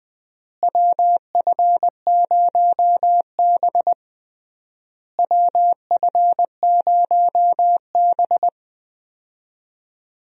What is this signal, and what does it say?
Morse code, heard as a keyed tone, about 21 wpm, 700 Hz. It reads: WF0B WF0B
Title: USA Call Signs - 4 Characters - Repeated 3x - 20wpm